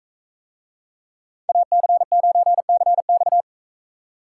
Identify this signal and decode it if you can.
Morse code, heard as a keyed tone, about 41 wpm, 700 Hz. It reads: AC9CX